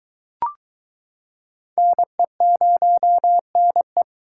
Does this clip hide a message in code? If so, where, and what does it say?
Yes: Morse code from 1.8 to 4.0 s, heard as a keyed tone, about 23 wpm, 700 Hz. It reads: NE0NE